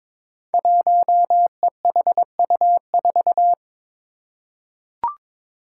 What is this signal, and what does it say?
Morse code, heard as a keyed tone, about 22 wpm, 700 Hz. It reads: 1EHU4
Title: Five Challenging Letter-Number Combinations 22wpm